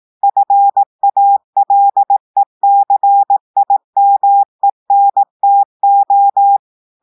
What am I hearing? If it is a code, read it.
Morse code, heard as a keyed tone, about 18 wpm, 800 Hz. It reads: FALECIMENTO
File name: audios / portugais / portugais texte en clair 1 vitesse 18 mots minutes